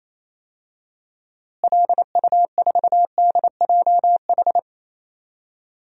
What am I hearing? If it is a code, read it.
Morse code, heard as a keyed tone, about 28 wpm, 700 Hz. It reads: LU4DJH
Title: DXCC-100 Call Signs - 6 Characters - Repeated 2x - 28wpm